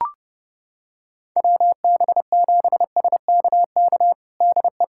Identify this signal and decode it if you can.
Morse code, heard as a keyed tone, about 30 wpm, 700 Hz. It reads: WB7SKK DE
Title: USA Callsign DE Callsign - 6 Characters - 30wpm